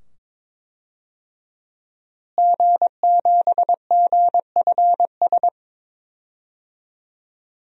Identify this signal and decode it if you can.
Morse code, heard as a keyed tone, about 22 wpm, 700 Hz. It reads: G7GFS
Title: DXCC-100 Call Signs - 5 Characters - Repeated 2x - 22wpm